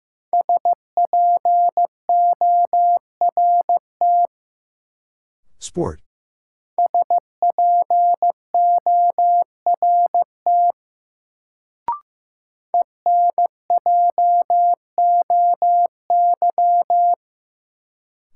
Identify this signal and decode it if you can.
Morse code, heard as a keyed tone, about 15 wpm, 700 Hz. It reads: SPORT SPORT ENJOY
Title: Sentences from Top 1000 Words - Review Part 4 - New Word Drill - 15wpm